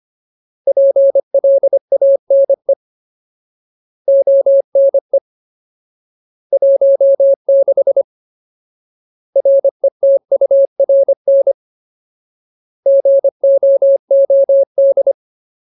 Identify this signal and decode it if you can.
Morse code, heard as a keyed tone, about 25 wpm, 550 Hz. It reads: PLANE ONE 16 RETURN GOOD
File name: Ultimate Word List / 25WPM Ultimate